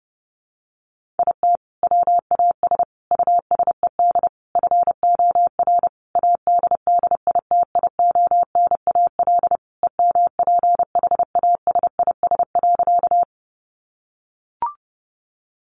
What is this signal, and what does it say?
Morse code, heard as a keyed tone, about 28 wpm, 700 Hz. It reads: IT WAS USED FOR ADDITIONAL EMPHASIS.
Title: Sentences from Top 1400 Words 30wpm